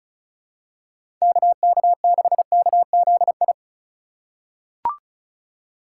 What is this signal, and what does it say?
Morse code, heard as a keyed tone, about 36 wpm, 700 Hz. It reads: KK6KZI
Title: USA Call Signs - 6 Characters - 4x Character Spacing - 35wpm